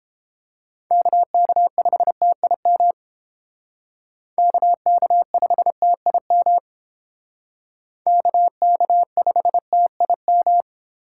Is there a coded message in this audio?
Yes: Morse code from 0.9 to 10.6 s, heard as a keyed tone, about 29 wpm, 700 Hz. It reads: KK5TIM KK5TIM KK5TIM